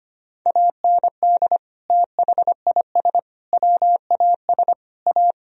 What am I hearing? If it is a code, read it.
Morse code, heard as a keyed tone, about 25 wpm, 700 Hz. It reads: AND THIS WAS A
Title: Sentences from Top 1000 Words - Review Part 3 - 25wpm